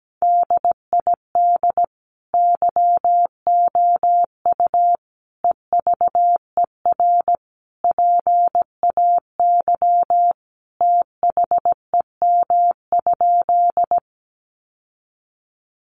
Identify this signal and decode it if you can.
Morse code, heard as a keyed tone, about 17 wpm, 700 Hz. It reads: DID YOU EVER PAY THEM?
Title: Sentences from Top 700 Words 17wpm